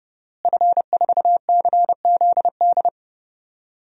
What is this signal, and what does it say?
Morse code, heard as a keyed tone, about 32 wpm, 700 Hz. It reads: F4CZD
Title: DXCC-100 Call Signs - 5 Characters - Repeated 2x - 30wpm